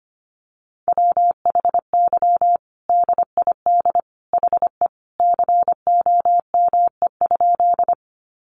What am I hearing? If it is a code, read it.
Morse code, heard as a keyed tone, about 25 wpm, 700 Hz. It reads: WHY DID HE COME?
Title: Sentences from Top 200 Words 25wpm